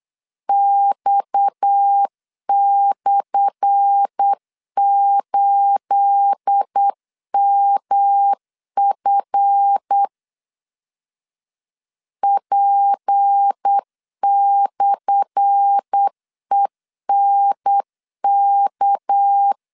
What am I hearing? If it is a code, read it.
Morse code, heard as a keyed tone, about 8 wpm, 800 Hz. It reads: X/8MF P/ENK
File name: LEZIONI / 10 SIGNS / 10A SLASH 8wpm 3452